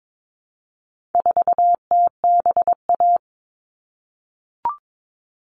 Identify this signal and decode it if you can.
Morse code, heard as a keyed tone, about 22 wpm, 700 Hz. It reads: 4TBA
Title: All Cut Numbers - 4-Digit 22wpm